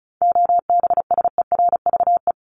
Morse code, heard as a keyed tone, about 34 wpm, 700 Hz. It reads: OBSERVE
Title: ED Suffix - Top 100 Words with Base Words 35wpm